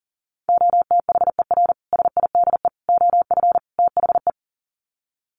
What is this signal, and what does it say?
Morse code, heard as a keyed tone, about 37 wpm, 700 Hz. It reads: OTHER SIDE OF THE
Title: Sets of 4 Words 40wpm